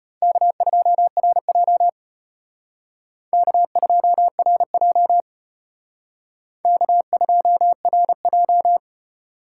Morse code, heard as a keyed tone, about 29 wpm, 700 Hz. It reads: K2RJ K2RJ K2RJ
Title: Speed Racing - USA 1x2 Call Signs 25wpm